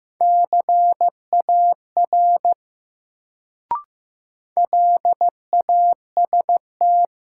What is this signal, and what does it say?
Morse code, heard as a keyed tone, about 15 wpm, 700 Hz. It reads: CAR LAST
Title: Sentences from Top 300 Words - New Word Drill - 15wpm